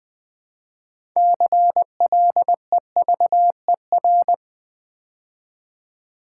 Morse code, heard as a keyed tone, about 20 wpm, 690 Hz. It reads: CLEVER
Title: Build A Word - 2 Letters At A Time - 6-Letter Words - 20wpm